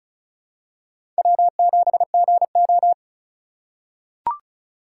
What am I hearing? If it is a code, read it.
Morse code, heard as a keyed tone, about 36 wpm, 700 Hz. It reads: W7GO
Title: USA Call Signs - 4 Characters - 3x Character Spacing - 35wpm